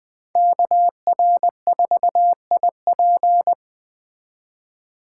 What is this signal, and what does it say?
Morse code, heard as a keyed tone, about 20 wpm, 700 Hz. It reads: KR4IP
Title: Speed Racing - USA 2x2 Call Signs 17wpm